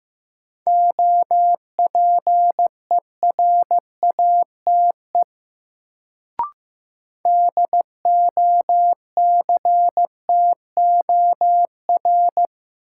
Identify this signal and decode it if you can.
Morse code, heard as a keyed tone, about 15 wpm, 700 Hz. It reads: OPERATE DOCTOR
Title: Sentences from Top 1500 Words - New Word Drill - 15wpm